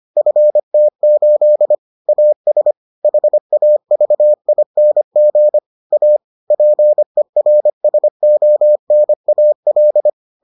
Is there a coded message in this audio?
Yes: Morse code from 0.2 to 10.1 s, heard as a keyed tone, about 25 wpm, 600 Hz. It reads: FT8 AS HAVING A PERSONAL